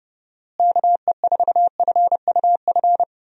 Morse code, heard as a keyed tone, about 30 wpm, 700 Hz. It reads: KE4FUF
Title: USA Call Signs - 6 Characters - Repeated 2x - 30wpm